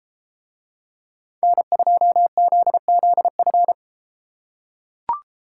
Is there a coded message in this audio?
Yes: Morse code from 1.4 to 3.7 s, heard as a keyed tone, about 31 wpm, 700 Hz. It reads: N2ZZF